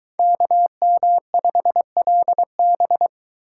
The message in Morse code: KM5LB

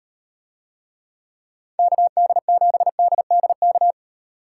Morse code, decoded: KD7DDK

